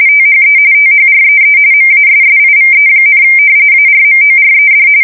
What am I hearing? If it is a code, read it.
RTTY, about 45.45 baud. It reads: THE THREE MASSIVE, ARMORED
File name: fsk170 / battleshipnj